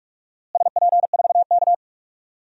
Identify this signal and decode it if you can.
Morse code, heard as a keyed tone, about 42 wpm, 700 Hz. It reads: SP4K
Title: DXCC-100 Call Signs - 4 Characters - Repeated 3x - 45wpm